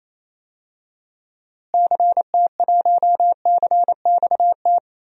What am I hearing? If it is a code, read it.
Morse code, heard as a keyed tone, about 28 wpm, 700 Hz. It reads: CT1CXT
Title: DXCC-100 Call Signs - 6 Characters - Repeated 3x - 28wpm